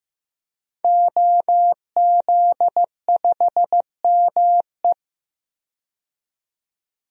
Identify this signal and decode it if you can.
Morse code, heard as a keyed tone, about 15 wpm, 700 Hz. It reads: OZ5ME
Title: DXCC-100 Call Signs - 5 Characters - 15wpm